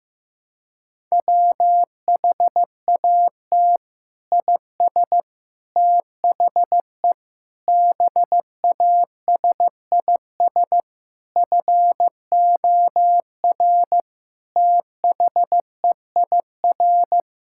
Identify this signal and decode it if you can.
Morse code, heard as a keyed tone, about 16 wpm, 700 Hz. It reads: WHAT IS THE BASIS FOR THEIR
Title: Sentences from Top 1000 Words - Review Part 6 - 15wpm